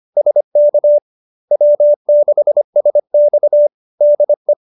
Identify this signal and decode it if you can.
Morse code, heard as a keyed tone, about 25 wpm, 590 Hz. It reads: SK W6SX DE